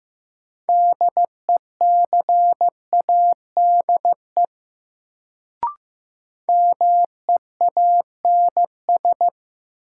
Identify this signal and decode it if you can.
Morse code, heard as a keyed tone, about 15 wpm, 700 Hz. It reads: DECADE MEANS
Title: Sets of 5 Words - New Word Drill - 15wpm